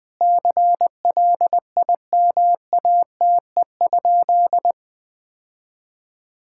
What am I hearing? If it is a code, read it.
Morse code, heard as a keyed tone, about 20 wpm, 700 Hz. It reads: CLIMATE?
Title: Sentences from Top 1600 Words 20wpm